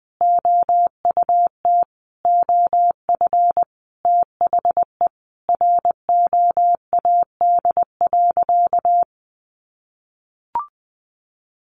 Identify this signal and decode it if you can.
Morse code, heard as a keyed tone, about 20 wpm, 700 Hz. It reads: OUT OF THE ROAD.
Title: Sentences from Top 400 Words 20wpm